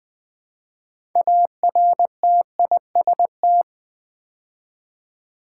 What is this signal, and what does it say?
Morse code, heard as a keyed tone, about 20 wpm, 700 Hz. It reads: ARTIST
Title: Build A Word - 3 Letters At A Time - 8-Letter Words - 20wpm